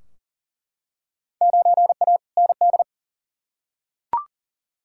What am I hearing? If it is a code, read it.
Morse code, heard as a keyed tone, about 40 wpm, 700 Hz. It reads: 9A ND
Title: Field Day Exchanges Encore - 40wpm